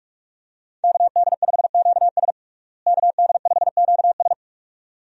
Morse code, heard as a keyed tone, about 46 wpm, 700 Hz. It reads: KD5QS KD5QS
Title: USA Call Signs - 5 Characters - Repeated 2x - 45wpm